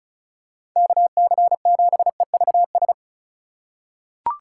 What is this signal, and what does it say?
Morse code, heard as a keyed tone, about 36 wpm, 700 Hz. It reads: KC7EVS
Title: USA Call Signs - 6 Characters - 2x Character Spacing - 35wpm